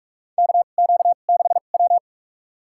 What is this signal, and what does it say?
Morse code, heard as a keyed tone, about 46 wpm, spaced as for 33 wpm, 700 Hz. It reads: KQ6W
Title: USA Call Signs - 4 Characters - 2x Character Spacing - 45wpm